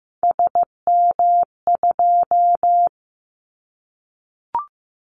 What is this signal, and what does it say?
Morse code, heard as a keyed tone, about 15 wpm, 700 Hz. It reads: SM2
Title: Callsign Prefixes for Top 25 DXCC - 15wpm